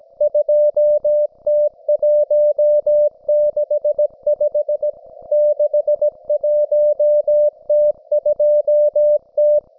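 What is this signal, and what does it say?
Morse code, heard as a keyed tone, about 17 wpm, 600 Hz. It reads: 2T165 61T2T
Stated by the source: Russian military Morse station, M32, RDL, on 5268 kHz.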